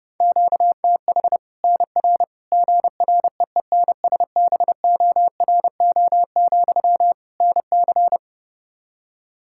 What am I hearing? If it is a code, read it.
Morse code, heard as a keyed tone, about 30 wpm, 700 Hz. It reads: QTH NR GREENSBORO, NC
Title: Practice QSO - Exchange 1 - Beginning - 30wpm